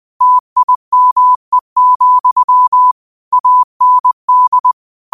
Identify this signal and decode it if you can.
Morse code, heard as a keyed tone, about 20 wpm, 1000 Hz. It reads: TIME, AND
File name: qotdfiles / 20wpm